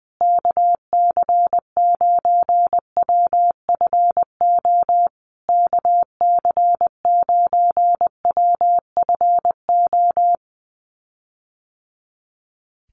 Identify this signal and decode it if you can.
Morse code, heard as a keyed tone, about 20 wpm, 700 Hz. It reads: KC9WFO KC9WFO
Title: USA Call Signs Repeated 3x 20wpm